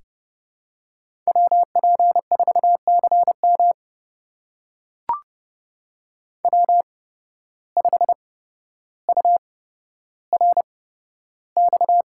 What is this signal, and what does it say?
Morse code, heard as a keyed tone, about 30 wpm, spaced as for 13 wpm, 700 Hz. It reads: WP4CM W5URX
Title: USA Call Signs - 8x Character Spacing - 30wpm